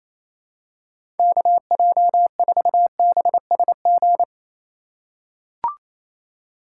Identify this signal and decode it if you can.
Morse code, heard as a keyed tone, about 28 wpm, 700 Hz. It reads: KJ4BSG
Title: USA Call Signs - 4x Character Spacing - 28wpm